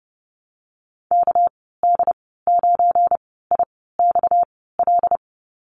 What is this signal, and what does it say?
Morse code, heard as a keyed tone, about 31 wpm, spaced as for 17 wpm, 700 Hz. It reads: KD9IXL
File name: Lesson 37g - Letter x - Call Signs 30wpm